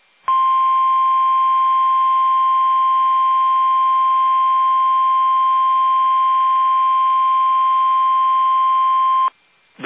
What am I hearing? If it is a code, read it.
Emergency Alert System (SAME) tones (the 1050 Hz attention tone). It is heard from 0.3 to 9.1 s.
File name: eas / KUSO / KUSO